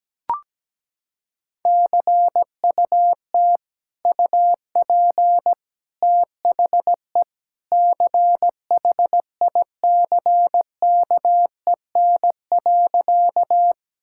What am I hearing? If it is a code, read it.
Morse code, heard as a keyed tone, about 17 wpm, 700 Hz. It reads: CUT UP THE CHICKEN.
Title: Sentences from Top 1000 Words - Review Part 5 - 17wpm